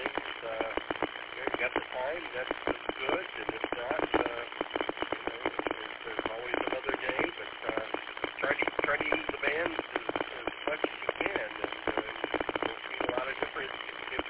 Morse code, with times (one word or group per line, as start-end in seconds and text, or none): none